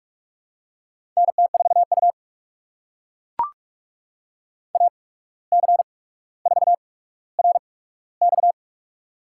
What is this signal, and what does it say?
Morse code, heard as a keyed tone, about 44 wpm, spaced as for 18 wpm, 700 Hz. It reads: NT4U AC4RX
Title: USA Call Signs - 8x Character Spacing - 45wpm